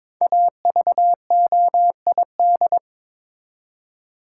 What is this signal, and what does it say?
Morse code, heard as a keyed tone, about 22 wpm, 700 Hz. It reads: AVOID
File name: 5 Letter Words - First 3 Letters Given - 22wpm